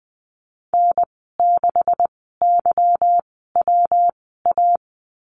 Morse code, heard as a keyed tone, about 20 wpm, spaced as for 14 wpm, 700 Hz. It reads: N6YWA
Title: 167 Lesson 27g - Number 7 - Call Signs 20wpm